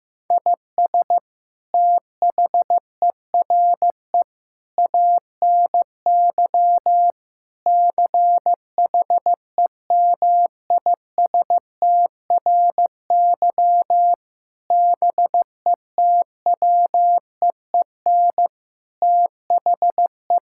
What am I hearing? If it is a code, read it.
Morse code, heard as a keyed tone, about 15 wpm, 700 Hz. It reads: IS THERE ANY CHEMISTRY BETWEEN THE
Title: Sentences from Top 1000 Words - Review Part 6 - 15wpm